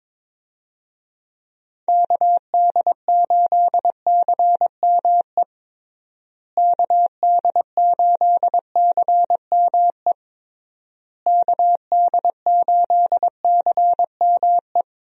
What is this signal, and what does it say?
Morse code, heard as a keyed tone, about 22 wpm, 700 Hz. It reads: KD8CME KD8CME KD8CME
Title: USA Call Signs - 6 Characters - Repeated 3x - 22wpm